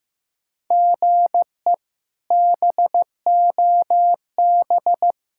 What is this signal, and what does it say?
Morse code, heard as a keyed tone, about 15 wpm, 700 Hz. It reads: GE BOB
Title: K1USN SST QSO Practice - 15wpm